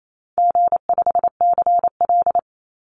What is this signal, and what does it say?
Morse code, heard as a keyed tone, about 28 wpm, 700 Hz. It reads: G5CL
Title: Call Signs Repeated 2x - 28wpm